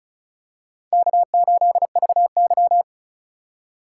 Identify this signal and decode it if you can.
Morse code, heard as a keyed tone, about 36 wpm, 700 Hz. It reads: K8VY
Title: USA Call Signs - 4 Characters - 2x Character Spacing - 35wpm